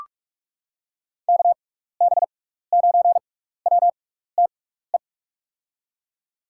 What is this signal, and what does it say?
Morse code, heard as a keyed tone, about 46 wpm, spaced as for 15 wpm, 700 Hz. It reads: KB9WTE
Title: USA Call Signs - 6 Characters - 6x Character Spacing - 45wpm